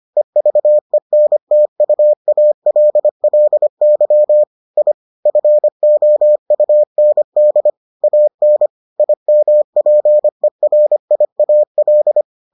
Morse code, heard as a keyed tone, about 25 wpm, 600 Hz. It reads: EVENTUALLY I FOUND AN IMPERIAL